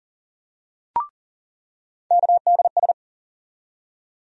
Morse code, heard as a keyed tone, about 40 wpm, 700 Hz. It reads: KDS
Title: USA Call Sign Postfixes - 3 Letters 40wpm